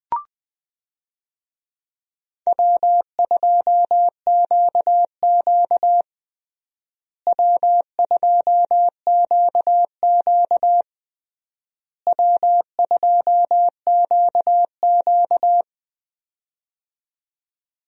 Morse code, heard as a keyed tone, about 20 wpm, 700 Hz. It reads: W2QQ W2QQ W2QQ
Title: USA Call Signs - 4 Characters - Repeated 3x - 20wpm